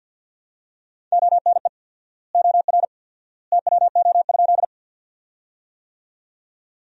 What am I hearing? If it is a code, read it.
Morse code, heard as a keyed tone, about 50 wpm, 700 Hz. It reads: ONE OR TWO?
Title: 4x Word-Spacing - Sentences from Top 100 Words - 50wpm